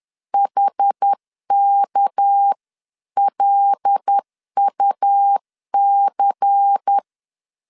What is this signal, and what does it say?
Morse code, heard as a keyed tone, about 10 wpm, 800 Hz. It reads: HKLUC
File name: CW / DET / 1 6 1500C S C 10wpm 4252